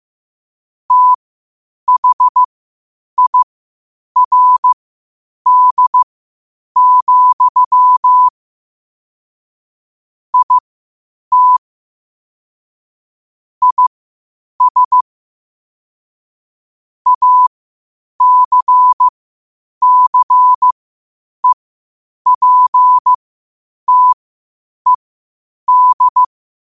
Morse code, heard as a keyed tone, about 15 wpm, spaced as for 8 wpm, 1000 Hz. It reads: THIRD, IT IS ACCEPTED
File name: qotdfiles / 8wpm